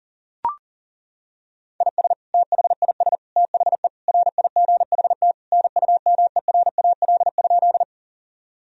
Morse code, heard as a keyed tone, about 37 wpm, 700 Hz. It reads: IS THIS THE RIGHT NUMERAL?
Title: Sentences from Top 1000 Words - Review Part 2 - 40wpm